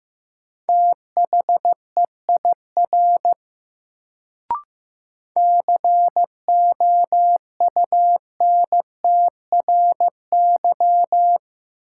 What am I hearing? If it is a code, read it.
Morse code, heard as a keyed tone, about 15 wpm, 700 Hz. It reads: THEIR COUNTRY
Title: Sets of 2 Words - New Word Drill - 15wpm